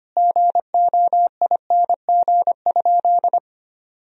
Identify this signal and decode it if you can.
Morse code, heard as a keyed tone, about 25 wpm, 700 Hz. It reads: GOING?
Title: Sentences from Top 1000 Words - Review Part 2 - 25wpm